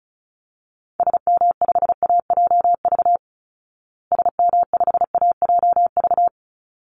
Morse code, heard as a keyed tone, about 34 wpm, 700 Hz. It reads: SM5AJV SM5AJV
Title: Call Signs Repeated 3x - 35wpm